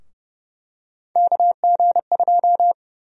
Morse code, heard as a keyed tone, about 31 wpm, 700 Hz. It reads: KG2
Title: USA Call Sign Prefixes - Repeated 2x - 30wpm